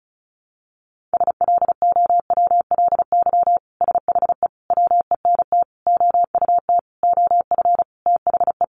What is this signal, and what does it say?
Morse code, heard as a keyed tone, about 34 wpm, 700 Hz. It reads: SLOWLY SHE WENT OUT OF THE
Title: Sentences from Top 1100 Words 35wpm